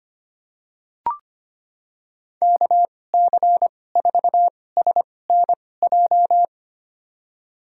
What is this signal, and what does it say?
Morse code, heard as a keyed tone, about 25 wpm, spaced as for 18 wpm, 700 Hz. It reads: KC4SNJ
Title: USA Call Signs - 6 Characters - 2x Character Spacing - 25wpm